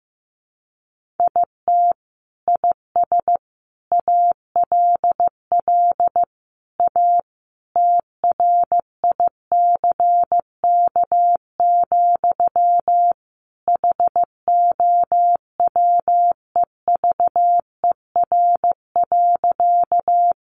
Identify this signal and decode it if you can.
Morse code, heard as a keyed tone, about 14 wpm, 700 Hz. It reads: IT IS ALL A TRICK, HOWEVER.
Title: Sentences from Top 1500 Words 15wpm